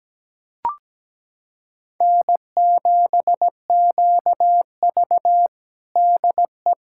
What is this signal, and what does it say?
Morse code, heard as a keyed tone, about 17 wpm, 700 Hz. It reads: N7QV DE
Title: USA Callsign DE Callsign - 4 Characters - 17wpm